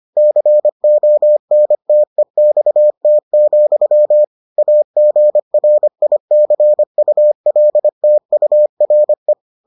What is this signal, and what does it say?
Morse code, heard as a keyed tone, about 25 wpm, 600 Hz. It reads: CONTEXT, AGRICULTURE